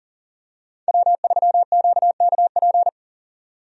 Morse code, heard as a keyed tone, about 37 wpm, 700 Hz. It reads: W3QKP
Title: USA Call Signs - 5 Characters - 40wpm